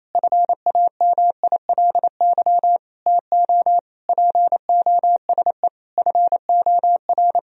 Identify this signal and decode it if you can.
Morse code, heard as a keyed tone, about 28 wpm, 700 Hz. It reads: FAMILY TO POSE FOR
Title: Sentences from Top 1000 Words - Review Part 2 - 28wpm